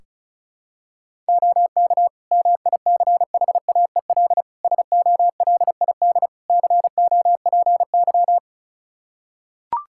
Morse code, heard as a keyed tone, about 36 wpm, 700 Hz. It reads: OK MICHAEL SOLID COPY
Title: Practice QSO - Exchange 3 - More Info - 35wpm